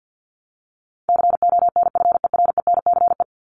Morse code, heard as a keyed tone, about 50 wpm, 700 Hz. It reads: CONFERENCE
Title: Top 1751-2000 Words 50wpm